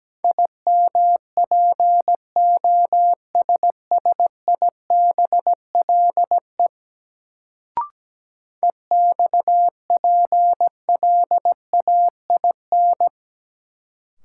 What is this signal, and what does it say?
Morse code, heard as a keyed tone, about 17 wpm, 700 Hz. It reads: IMPOSSIBLE EXPLAIN
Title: Sentences from Top 1100 Words - New Word Drill - 17wpm